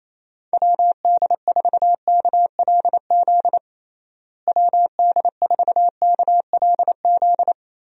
Morse code, heard as a keyed tone, about 28 wpm, 700 Hz. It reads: WD4KLZ WD4KLZ
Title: USA Call Signs - 6 Characters - Repeated 3x - 28wpm